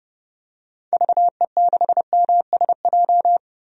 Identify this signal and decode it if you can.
Morse code, heard as a keyed tone, about 30 wpm, 700 Hz. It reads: VE6MSJ
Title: DXCC-100 Call Signs - 6 Characters - Repeated 3x - 30wpm